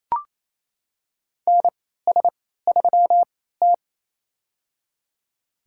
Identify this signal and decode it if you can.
Morse code, heard as a keyed tone, about 27 wpm, spaced as for 16 wpm, 700 Hz. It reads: NS3T
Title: USA Call Signs - 3x Character Spacing - 28wpm